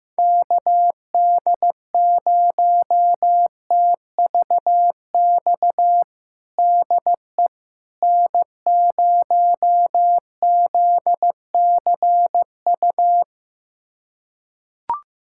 Morse code, heard as a keyed tone, about 15 wpm, 700 Hz. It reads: KD0TVX DE N0ZCU
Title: USA Callsign DE Callsign 15wpm